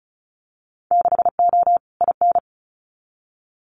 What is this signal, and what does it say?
Morse code, heard as a keyed tone, about 34 wpm, 700 Hz. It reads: 6O IN
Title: Winter Field Day Exchanges Encore 35wpm